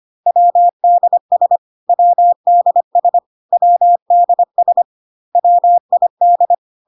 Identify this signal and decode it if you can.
Morse code, heard as a keyed tone, about 25 wpm, 700 Hz. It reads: WDS WDS WDS WID
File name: CW abbreviations alphabetical repeated 3 times 25 25wpm